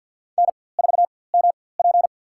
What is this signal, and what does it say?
Morse code, heard as a keyed tone, about 53 wpm, spaced as for 24 wpm, 700 Hz. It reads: N4MP
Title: USA Call Signs - 4x Character Spacing - 50wpm